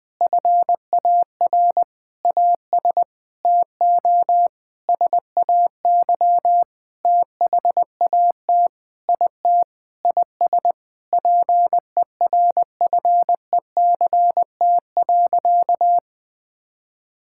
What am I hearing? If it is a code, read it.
Morse code, heard as a keyed tone, about 21 wpm, 700 Hz. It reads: FAR AS TO SAY THAT IT IS PERFECT.
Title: Sentences with 9-Word Phrases - Spoken Beginning - 20wpm